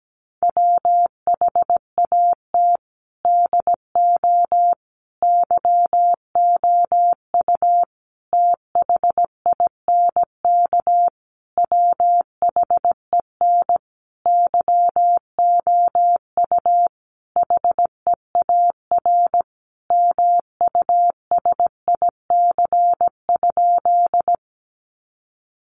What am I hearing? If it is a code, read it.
Morse code, heard as a keyed tone, about 17 wpm, 700 Hz. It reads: WHAT DO YOU THINK WHEN YOU HEAR MUSIC?
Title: Sentences from Top 300 Words 17wpm